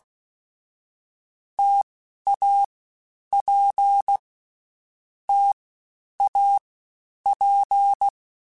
Morse code, heard as a keyed tone, about 16 wpm, 800 Hz. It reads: T A P T A P
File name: Difficult Combos / Difficult Combos-mp3s / T AOEI @15